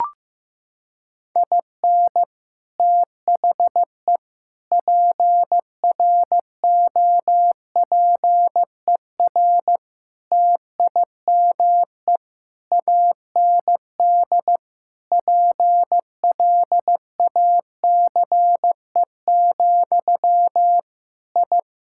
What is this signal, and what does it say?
Morse code, heard as a keyed tone, about 16 wpm, 700 Hz. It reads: IN THE PROPER TIME AND PLACE, I
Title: Sentences from Top 1000 Words - Review Part 5 - 15wpm